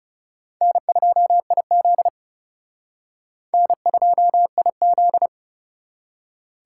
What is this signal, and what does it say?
Morse code, heard as a keyed tone, about 36 wpm, 700 Hz. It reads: N2IZ N2IZ
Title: Speed Racing - USA 1x2 Call Signs 30wpm